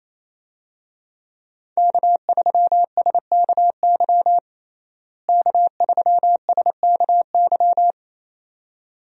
Morse code, heard as a keyed tone, about 28 wpm, 700 Hz. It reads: K3SKY K3SKY
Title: USA Call Signs - 5 Characters - Repeated 2x - 28wpm